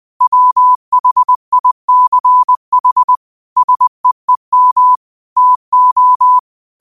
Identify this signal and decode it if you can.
Morse code, heard as a keyed tone, about 20 wpm, 1000 Hz. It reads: WHICH SEEM TO
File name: qotdfiles / 20wpm